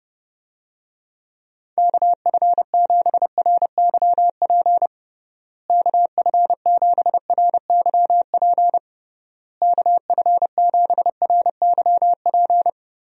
Morse code, heard as a keyed tone, about 30 wpm, 700 Hz. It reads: KF7RYP KF7RYP KF7RYP
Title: USA Call Signs - 6 Characters - Repeated 3x - 30wpm